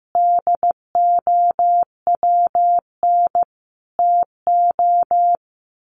Morse code, heard as a keyed tone, about 15 wpm, 700 Hz. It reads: DOWN TO
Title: A Sound Of Thunder 15wpm